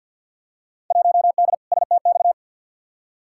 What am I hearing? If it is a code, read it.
Morse code, heard as a keyed tone, about 47 wpm, 700 Hz. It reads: 1D STX
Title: Field Day Exchanges - 50wpm